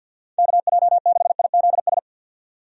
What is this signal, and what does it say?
Morse code, heard as a keyed tone, about 47 wpm, 700 Hz. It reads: KJ6IZS
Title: USA Call Signs - 6 Characters - 50wpm